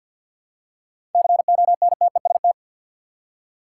Morse code, heard as a keyed tone, about 50 wpm, 700 Hz. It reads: CONTEST
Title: Seven Letter Words - 50wpm